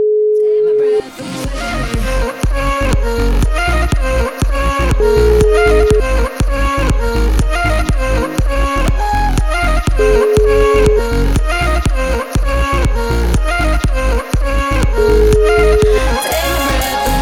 Ringback tone (Europe, 425 Hz), with noise.